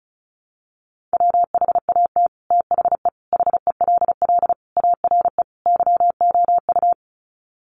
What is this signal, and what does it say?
Morse code, heard as a keyed tone, about 34 wpm, 700 Hz. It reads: WHAT THE HELL ARE YOU
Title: Sets of 5 Words Encore 35wpm